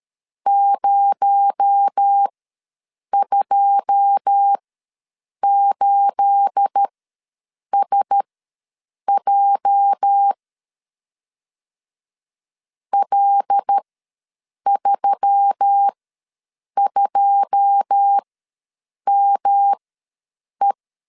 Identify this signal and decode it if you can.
Morse code, heard as a keyed tone, about 13 wpm, spaced as for 7 wpm, 800 Hz. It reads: 028SJ L32ME
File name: LEZIONI / 9 28 ws 12wpm 3354